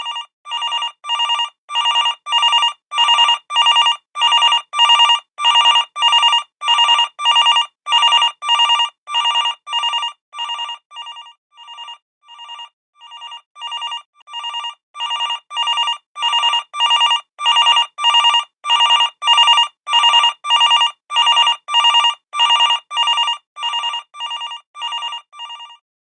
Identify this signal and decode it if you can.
There is no code.